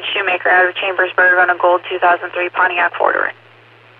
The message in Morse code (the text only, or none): none